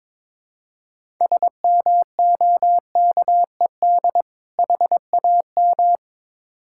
Morse code, heard as a keyed tone, about 22 wpm, 700 Hz. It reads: SMOKED HAM
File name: Difficult Words with Hints - 22wpm